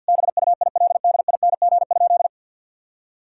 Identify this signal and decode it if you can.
Morse code, heard as a keyed tone, about 53 wpm, 700 Hz. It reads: BUILDING?
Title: Sentences from Top 1000 Words - Review Part 5 - 50wpm